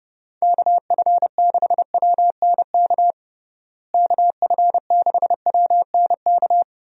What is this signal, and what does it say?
Morse code, heard as a keyed tone, about 30 wpm, 700 Hz. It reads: KF6WNK KF6WNK
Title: USA Call Signs - 6 Characters - Repeated 3x - 30wpm